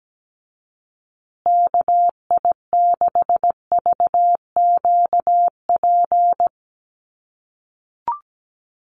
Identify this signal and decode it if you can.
Morse code, heard as a keyed tone, about 17 wpm, 700 Hz. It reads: KI6VQP
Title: USA Call Signs Repeated 3x 17wpm